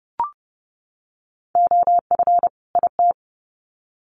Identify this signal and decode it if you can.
Morse code, heard as a keyed tone, about 30 wpm, 700 Hz. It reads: OF IT
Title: Sets of 2 Words 30wpm